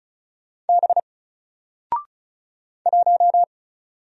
Morse code, heard as a keyed tone, about 36 wpm, 700 Hz. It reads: B 1
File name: Single-Letter-Number-35wpm